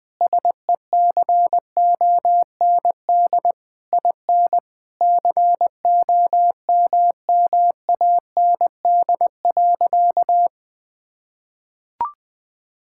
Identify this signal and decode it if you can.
Morse code, heard as a keyed tone, about 20 wpm, 700 Hz. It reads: SECOND IN COMMAND.
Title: Sentences from Top 1600 Words 20wpm